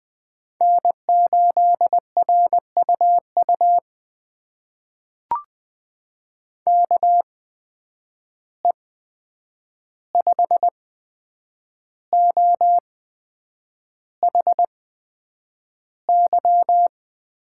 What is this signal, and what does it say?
Morse code, heard as a keyed tone, about 20 wpm, spaced as for 8 wpm, 700 Hz. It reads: N8RUU KE5OHY
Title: USA Call Signs - 8x Character Spacing - 20wpm